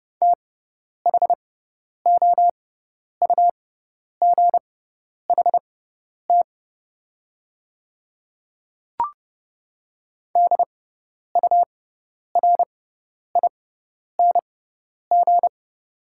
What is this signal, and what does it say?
Morse code, heard as a keyed tone, about 30 wpm, spaced as for 10 wpm, 700 Hz. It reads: THOUGHT DURING
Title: 6x Character Spacing - Top 500 Words - 30wpm